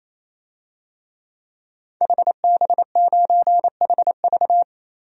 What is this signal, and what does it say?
Morse code, heard as a keyed tone, about 28 wpm, 700 Hz. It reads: HB9HV
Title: DXCC-100 Call Signs - 5 Characters - 28wpm